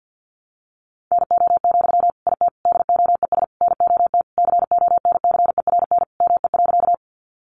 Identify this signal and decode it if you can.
Morse code, heard as a keyed tone, about 50 wpm, 700 Hz. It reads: NO, IT DOES NOT CONCERN ME.
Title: Sentences from Top 1000 Words 50wpm